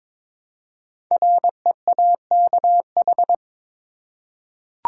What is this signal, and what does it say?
Morse code, heard as a keyed tone, about 22 wpm, 700 Hz. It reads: REAKH
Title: Code Groups - 5 Letter:Numbers - 22wpm